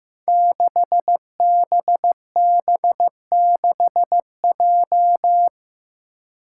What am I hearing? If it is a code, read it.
Morse code, heard as a keyed tone, about 15 wpm, 700 Hz. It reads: 6BB6J
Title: Five Challenging Letter-Number Combinations 15wpm